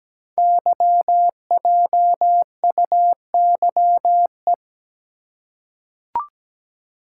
Morse code, heard as a keyed tone, about 17 wpm, 700 Hz. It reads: YJUYE
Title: Code Groups - 5 Letters - 17wpm